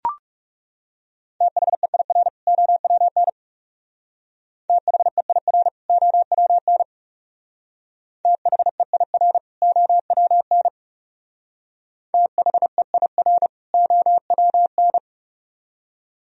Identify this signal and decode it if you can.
Morse code, heard as a keyed tone, about 37 wpm, 700 Hz. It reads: THEIR OWN THEIR OWN THEIR OWN THEIR OWN
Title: Speed-Racing - Sets of 2 Words 30wpm